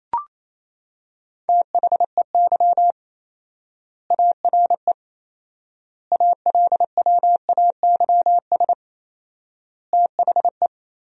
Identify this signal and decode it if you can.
Morse code, heard as a keyed tone, about 27 wpm, 700 Hz. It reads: THEY ARE ALWAYS THE
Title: 4x Word-Spacing - Sentences from Top 300 Words - 28wpm